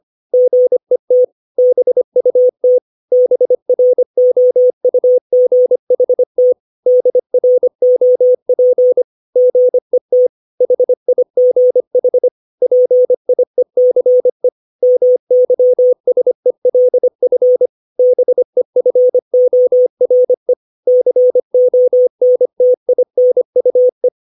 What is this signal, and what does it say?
Morse code, heard as a keyed tone, about 25 wpm, 500 Hz. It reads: GET BUT BROUGHT DROP GET HIGH PIECE MYSELF BEFORE CONTINUE
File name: 632 words / 632 Words 25WPM